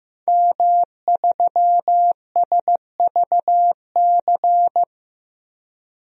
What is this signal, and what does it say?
Morse code, heard as a keyed tone, about 15 wpm, 700 Hz. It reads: M3SVC
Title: DXCC-100 Call Signs - 5 Characters - 15wpm